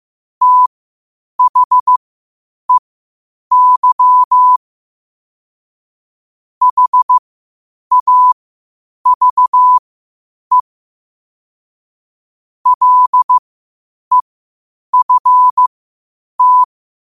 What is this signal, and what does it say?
Morse code, heard as a keyed tone, about 15 wpm, spaced as for 8 wpm, 1000 Hz. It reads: THEY HAVE LEFT
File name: qotdfiles / 8wpm